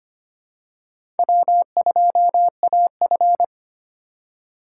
Morse code, heard as a keyed tone, about 25 wpm, 700 Hz. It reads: W2AF
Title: USA Call Signs - 4 Characters - 3x Character Spacing - 25wpm